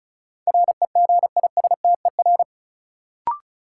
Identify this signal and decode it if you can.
Morse code, heard as a keyed tone, about 34 wpm, 700 Hz. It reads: REGISTER
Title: Sentences from Top 1200 Words - New Word Drill - 35wpm